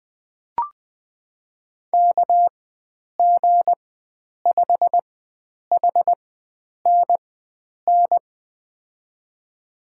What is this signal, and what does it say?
Morse code, heard as a keyed tone, about 21 wpm, spaced as for 9 wpm, 700 Hz. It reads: KG5HNN
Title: USA Call Signs - 6 Characters - 4x Character Spacing - 20wpm